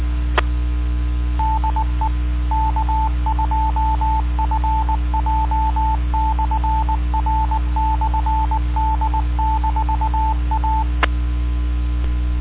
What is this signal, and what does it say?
Morse code, heard as a keyed tone, about 19 wpm, 880 Hz, with radio noise. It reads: E DE K2FJ/R/D-A